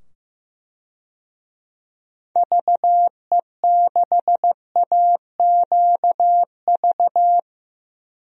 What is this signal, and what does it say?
Morse code, heard as a keyed tone, about 15 wpm, 700 Hz. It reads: VE6AQV
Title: DXCC-100 Call Signs - 6 Characters - 15wpm